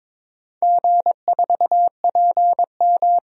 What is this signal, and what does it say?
Morse code, heard as a keyed tone, about 22 wpm, 700 Hz. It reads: G4PM
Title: DXCC-100 Call Signs - 4 Characters - 22wpm